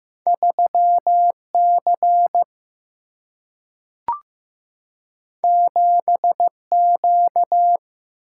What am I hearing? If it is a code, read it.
Morse code, heard as a keyed tone, about 15 wpm, 700 Hz. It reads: 3C 7Q
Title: Single-Letter Single-Number 15wpm